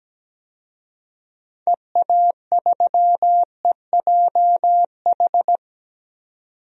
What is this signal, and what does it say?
Morse code, heard as a keyed tone, about 17 wpm, 700 Hz. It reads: EA3EJH
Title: DXCC-100 Call Signs - 6 Characters - 17wpm